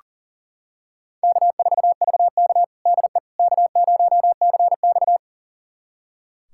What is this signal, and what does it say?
Morse code, heard as a keyed tone, about 40 wpm, 700 Hz. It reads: K4VK DE K0CX
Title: USA Callsign DE Callsign - 4 Characters - 40wpm